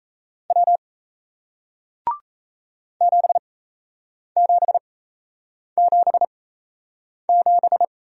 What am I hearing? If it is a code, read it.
Morse code, heard as a keyed tone, about 37 wpm, spaced as for 8 wpm, 700 Hz. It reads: W 7777